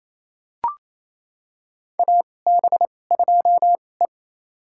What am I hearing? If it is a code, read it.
Morse code, heard as a keyed tone, about 28 wpm, spaced as for 20 wpm, 700 Hz. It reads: AB2E
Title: USA Call Signs - 2x Character Spacing - 28wpm